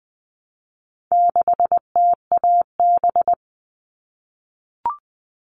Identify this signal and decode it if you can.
Morse code, heard as a keyed tone, about 20 wpm, 700 Hz. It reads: 6TAB
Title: All Cut Numbers - 4-Digit 20wpm